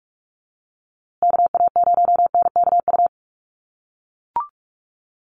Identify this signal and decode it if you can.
Morse code, heard as a keyed tone, about 44 wpm, 700 Hz. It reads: KA0NKU
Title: USA Call Signs Repeated 3x 45wpm